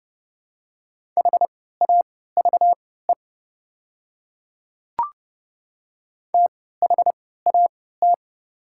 Morse code, heard as a keyed tone, about 30 wpm, spaced as for 17 wpm, 700 Hz. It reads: HAVE THAT
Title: 3x Character Spacing - Top 100 Words - 30wpm